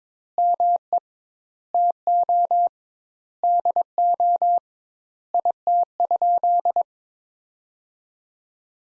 Morse code, heard as a keyed tone, about 22 wpm, 700 Hz. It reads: ME TO DO IT?
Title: Sentences Top 200 Words - 22wpm